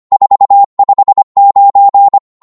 Morse code, heard as a keyed tone, about 25 wpm, 800 Hz. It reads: 459
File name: qso-25wpm